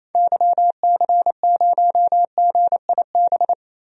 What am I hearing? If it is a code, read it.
Morse code, heard as a keyed tone, about 28 wpm, 700 Hz. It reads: YC0GIB
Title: DXCC-100 Call Signs - 6 Characters - 28wpm